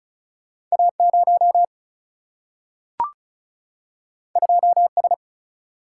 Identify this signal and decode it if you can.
Morse code, heard as a keyed tone, about 36 wpm, 700 Hz. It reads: A 0 2 S
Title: Single-Letter Single-Number 35wpm